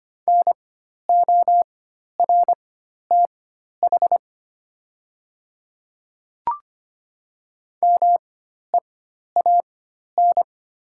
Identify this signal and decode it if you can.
Morse code, heard as a keyed tone, about 25 wpm, spaced as for 12 wpm, 700 Hz. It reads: NORTH MEAN
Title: 4x Character Spacing - Top 300 Words - 25wpm